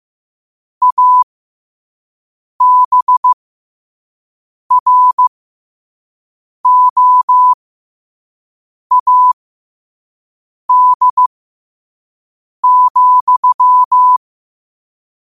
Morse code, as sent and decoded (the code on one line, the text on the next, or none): .- -... .-. --- .- -.. --..--
ABROAD,